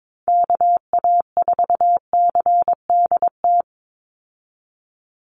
Morse code, heard as a keyed tone, about 22 wpm, 700 Hz. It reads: KA4CDT
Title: USA Call Signs 22wpm